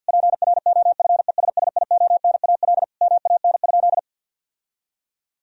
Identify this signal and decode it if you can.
Morse code, heard as a keyed tone, about 50 wpm, 700 Hz. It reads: PROFESSIONAL MAN?
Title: Sentences from Top 1000 Words - Review Part 4 - 50wpm